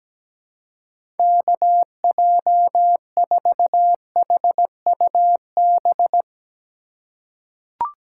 Morse code, heard as a keyed tone, about 17 wpm, 700 Hz. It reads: KJ4HUB
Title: USA Call Signs - 6 Characters - 3x Character Spacing - 17wpm